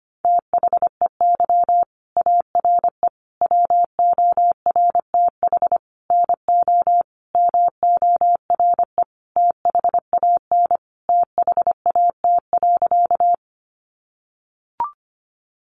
Morse code, heard as a keyed tone, about 25 wpm, 700 Hz. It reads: THEY ARE WORTH NO MORE THAN THAT.
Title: Sentences from Top 800 Words 25wpm